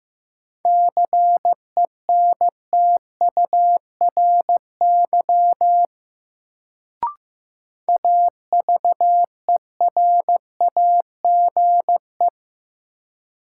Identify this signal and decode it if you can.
Morse code, heard as a keyed tone, about 15 wpm, 700 Hz. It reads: CENTURY AVERAGE
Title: Sets of 5 Words - New Word Drill - 15wpm